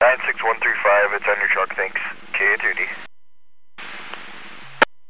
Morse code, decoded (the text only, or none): none